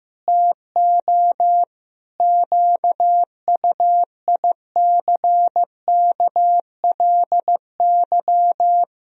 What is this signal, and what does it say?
Morse code, heard as a keyed tone, about 15 wpm, 700 Hz. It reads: TO QUICKLY
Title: Sentences from Top 1000 Words - Review Part 4 - 15wpm